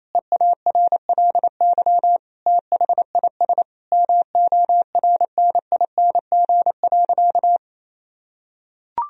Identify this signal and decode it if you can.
Morse code, heard as a keyed tone, about 27 wpm, 700 Hz. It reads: EARLY THIS MORNING.
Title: Sentences from Top 1000 Words - Review Part 2 - 28wpm